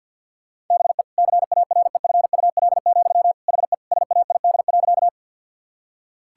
Morse code, heard as a keyed tone, about 50 wpm, 700 Hz. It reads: BE CAREFUL, HE SAID.
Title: Sentences from Top 1600 Words 50wpm